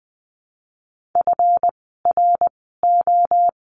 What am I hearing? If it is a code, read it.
Morse code, heard as a keyed tone, about 20 wpm, spaced as for 14 wpm, 700 Hz. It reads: FRO
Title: 139 Lesson 23f - Letter f - Words 20wpm